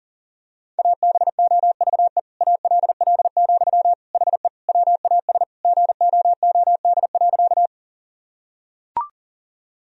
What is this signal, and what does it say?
Morse code, heard as a keyed tone, about 37 wpm, 700 Hz. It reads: ABOVE ALL, HE WAS GOOD.
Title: Sentences from Top 1000 Words - Review Part 2 - 40wpm